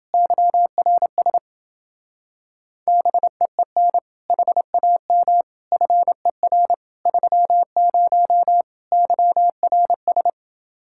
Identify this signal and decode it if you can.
Morse code, heard as a keyed tone, about 30 wpm, 700 Hz. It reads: YRS BEEN HAM FER 30 YRS